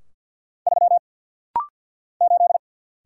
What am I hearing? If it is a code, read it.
Morse code, heard as a keyed tone, about 49 wpm, 700 Hz. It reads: 3 8
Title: Single Letter-Number - Rapid-Fire 50wpm